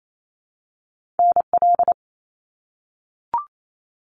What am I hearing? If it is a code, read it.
Morse code, heard as a keyed tone, about 28 wpm, 700 Hz. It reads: NL
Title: Canadian Provinces and Territories - Repeated 3x - 28wpm